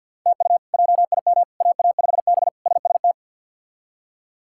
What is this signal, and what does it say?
Morse code, heard as a keyed tone, about 46 wpm, 700 Hz. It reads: TU JIM AA5B SST
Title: K1USN SST QSO Practice - 50wpm